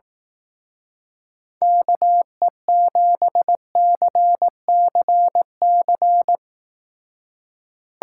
Morse code, heard as a keyed tone, about 19 wpm, 700 Hz. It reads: KE7CCC
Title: Speed Racing - USA 2x3 Call Signs 15wpm